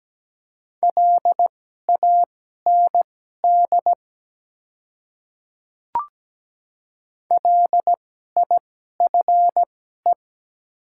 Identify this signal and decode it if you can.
Morse code, heard as a keyed tone, about 17 wpm, spaced as for 12 wpm, 690 Hz. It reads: LAND LIFE
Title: 2x Character Spacing - Top 300 Words - 17wpm